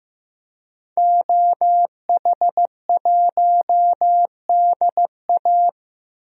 Morse code, heard as a keyed tone, about 15 wpm, 700 Hz. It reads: OH1DA
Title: DXCC-100 Call Signs - 5 Characters - Repeated 2x - 15wpm